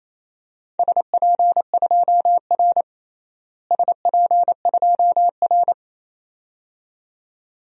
Morse code, heard as a keyed tone, about 28 wpm, 700 Hz. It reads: SP2R SP2R
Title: DXCC-100 Call Signs - 4 Characters - Repeated 2x - 28wpm